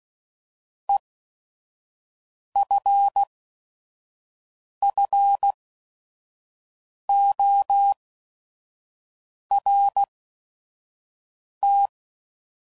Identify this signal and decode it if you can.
Morse code, heard as a keyed tone, about 16 wpm, spaced as for 5 wpm, 800 Hz. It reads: EFFORT